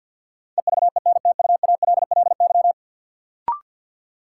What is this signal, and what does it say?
Morse code, heard as a keyed tone, about 50 wpm, 700 Hz. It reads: EVENTUALLY
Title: Sentences from Top 1000 Words - Review Part 6 - New Word Drill - 50wpm